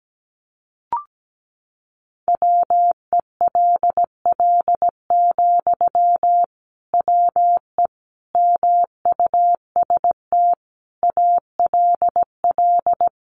Morse code, heard as a keyed tone, about 17 wpm, 700 Hz. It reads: WELL, WE MUST ALL
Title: Sentences from Top 1400 Words 17wpm